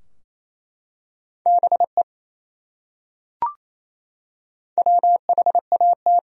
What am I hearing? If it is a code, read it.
Morse code, heard as a keyed tone, about 28 wpm, 700 Hz. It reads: BE WHAT
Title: Top 100 Words Repeated 28wpm